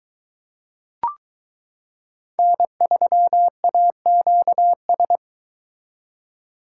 Morse code, heard as a keyed tone, about 22 wpm, 700 Hz. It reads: N3AQS